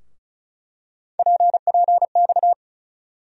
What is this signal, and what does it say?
Morse code, heard as a keyed tone, about 36 wpm, 700 Hz. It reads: PPX
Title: USA Call Sign Postfixes - 3 Letters 35wpm